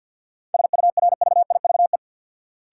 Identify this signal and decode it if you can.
Morse code, heard as a keyed tone, about 52 wpm, 700 Hz. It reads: SURVIVE
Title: Sentences from Top 1300 Words - New Word Drill Alt - 50wpm